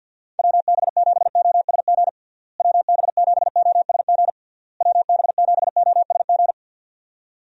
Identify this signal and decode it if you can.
Morse code, heard as a keyed tone, about 48 wpm, 700 Hz. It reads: WB7OSG WB7OSG WB7OSG
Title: USA Call Signs - 6 Characters - Repeated 3x - 50wpm